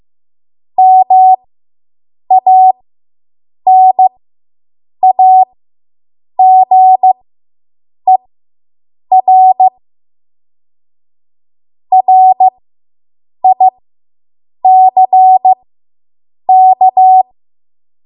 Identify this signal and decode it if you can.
Morse code, heard as a keyed tone, about 15 wpm, spaced as for 7 wpm, 750 Hz. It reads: MANAGER RICK